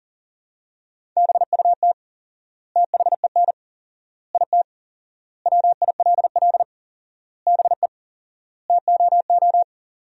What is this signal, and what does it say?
Morse code, heard as a keyed tone, about 40 wpm, 700 Hz. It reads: BUT THEN IT WILL BE TOO
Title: 4x Word-Spacing - Sentences from Top 300 Words - 40wpm